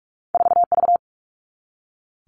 Morse code, heard as a keyed tone, about 46 wpm, 700 Hz. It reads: 4V
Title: All Cut Numbers - 2-Digit 45wpm